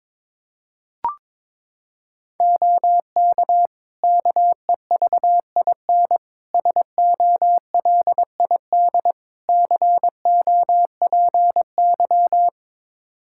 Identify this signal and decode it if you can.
Morse code, heard as a keyed tone, about 22 wpm, 700 Hz. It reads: OK KEVIN SOLID COPY
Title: Practice QSO - Exchange 3 - More Info - 22wpm